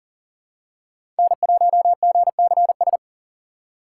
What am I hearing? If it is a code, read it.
Morse code, heard as a keyed tone, about 40 wpm, 700 Hz. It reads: N1GCS
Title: USA Call Signs - 5 Characters - 4x Character Spacing - 40wpm